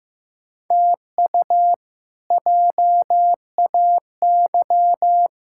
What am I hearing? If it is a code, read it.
Morse code, heard as a keyed tone, about 15 wpm, 700 Hz. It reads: TU JAY
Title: K1USN SST QSO Practice - 15wpm